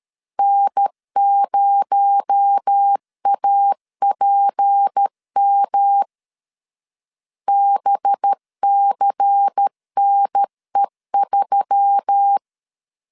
Morse code, heard as a keyed tone, about 13 wpm, 800 Hz. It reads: N0APM BCNE3
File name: CW / DET / 1 6 1500C S A 12wpm 3440